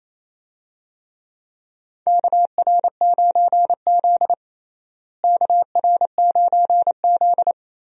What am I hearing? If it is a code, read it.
Morse code, heard as a keyed tone, about 28 wpm, 700 Hz. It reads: KR9Z KR9Z
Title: USA Call Signs - 4 Characters - Repeated 3x - 28wpm